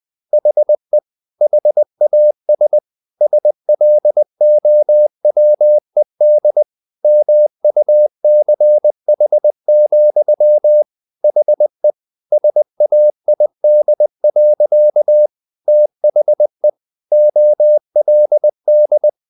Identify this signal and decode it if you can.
Morse code, heard as a keyed tone, about 20 wpm, 600 Hz. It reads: HE HAS SLOWED MUCH, HE SAID. THE OLD
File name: the old man and the sea / the old man and the sea part 1 20wpm